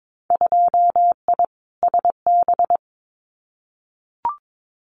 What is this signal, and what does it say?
Morse code, heard as a keyed tone, about 22 wpm, 700 Hz. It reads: 2I SB
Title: Winter Field Day Exchanges Encore 22wpm